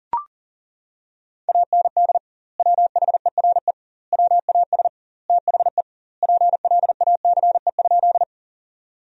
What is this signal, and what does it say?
Morse code, heard as a keyed tone, about 37 wpm, 700 Hz. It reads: AND WHERE WAS THE PLACE?
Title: 2x Word-Spacing - Sentences from Top 200 Words - 40wpm